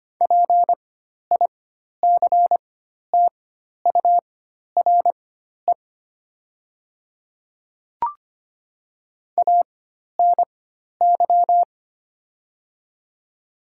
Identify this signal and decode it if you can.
Morse code, heard as a keyed tone, about 25 wpm, spaced as for 12 wpm, 700 Hz. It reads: PICTURE ANY
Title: 4x Character Spacing - Top 200 Words - 25wpm